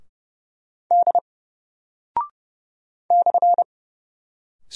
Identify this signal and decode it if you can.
Morse code, heard as a keyed tone, about 30 wpm, 700 Hz. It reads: D /
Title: Single Character (Letter:Number:Punctuation:Prosign) 30wpm